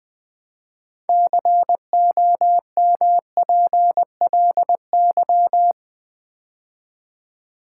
Morse code, heard as a keyed tone, about 20 wpm, 700 Hz. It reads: COMPLY
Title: Build A Word - 2 Letters At A Time - 6-Letter Words - 20wpm